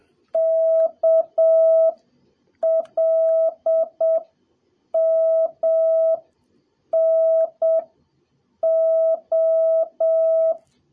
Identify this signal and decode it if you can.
Morse code, heard as a keyed tone, about 7 wpm, spaced as for 6 wpm, 650 Hz. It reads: KLMNO
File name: jh8cbh / 2021cw / cw03a